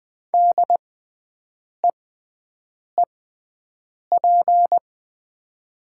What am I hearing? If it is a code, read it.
Morse code, heard as a keyed tone, about 20 wpm, spaced as for 7 wpm, 700 Hz. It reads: DEEP
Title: 6x Character Spacing - Top 500 Words - 20wpm